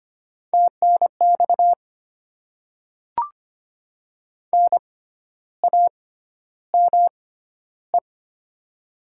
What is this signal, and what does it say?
Morse code, heard as a keyed tone, about 25 wpm, spaced as for 12 wpm, 700 Hz. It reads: TNX NAME
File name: Top 20 QSO Elements - 6x Character Spacing - 25wpm